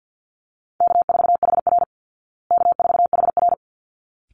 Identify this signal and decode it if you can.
Morse code, heard as a keyed tone, about 49 wpm, 700 Hz. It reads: K4HR K4HR
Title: Call Signs Repeated 3x - 50wpm